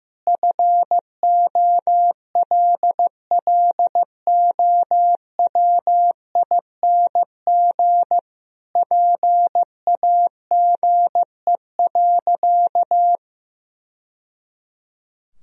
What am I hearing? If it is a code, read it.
Morse code, heard as a keyed tone, about 15 wpm, 700 Hz. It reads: FOLLOWING PAGE.
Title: Sentences from Top 1000 Words - Review Part 1 - 15wpm